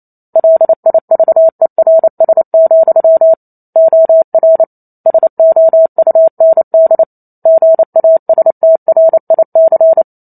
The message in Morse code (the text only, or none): LIVERS, OR SOUND GASTRIC